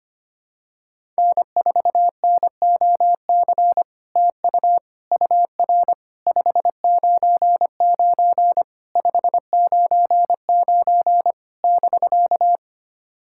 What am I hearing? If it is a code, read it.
Morse code, heard as a keyed tone, about 25 wpm, 700 Hz. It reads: N4NOC TU UR 599 599 <BK>
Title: SOTA QSO Practice 25wpm